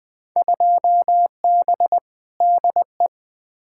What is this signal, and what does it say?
Morse code, heard as a keyed tone, about 21 wpm, 700 Hz. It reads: 2B DE
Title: Field Day Exchanges Encore - 20wpm